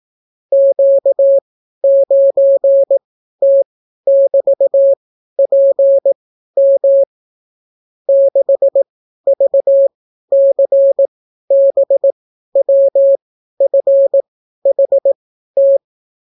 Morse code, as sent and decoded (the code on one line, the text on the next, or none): --.- ----. - -...- .--. -- / -.... ...- -.-. -... .-- ..-. .... -
Q9T<BT>PM 6VCBWFHT